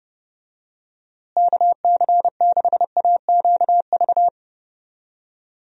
Morse code, heard as a keyed tone, about 32 wpm, 700 Hz. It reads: KC6AQV
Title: USA Call Signs - 6 Characters - Repeated 2x - 30wpm